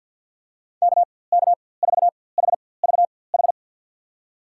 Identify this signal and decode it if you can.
Morse code, heard as a keyed tone, about 52 wpm, spaced as for 23 wpm, 700 Hz. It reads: KK4HVH
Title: USA Call Signs - 6 Characters - 4x Character Spacing - 50wpm